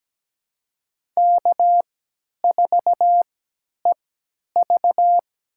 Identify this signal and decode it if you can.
Morse code, heard as a keyed tone, about 17 wpm, spaced as for 10 wpm, 700 Hz. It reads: K4EV
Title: USA Call Signs - 3x Character Spacing - 17wpm